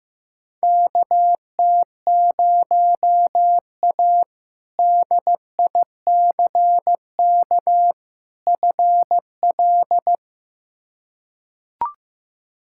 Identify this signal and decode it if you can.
Morse code, heard as a keyed tone, about 15 wpm, 700 Hz. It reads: KT0A DICK FL
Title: K1USN SST QSO Practice - 15wpm